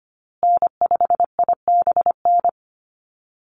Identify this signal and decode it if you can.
Morse code, heard as a keyed tone, about 25 wpm, 700 Hz. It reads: N5IBN
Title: USA Call Signs Repeated 2x 25wpm